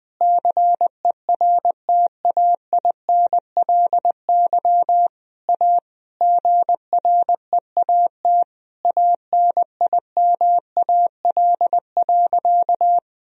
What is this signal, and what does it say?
Morse code, heard as a keyed tone, about 20 wpm, 700 Hz. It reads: CERTAINLY A GREAT ANIMAL.
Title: Sentences from Top 1000 Words - Review Part 1 - 20wpm